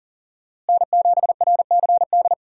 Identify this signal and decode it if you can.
Morse code, heard as a keyed tone, about 40 wpm, 700 Hz. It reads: N7RCD
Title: USA Call Signs - 5 Characters - Repeated 3x - 40wpm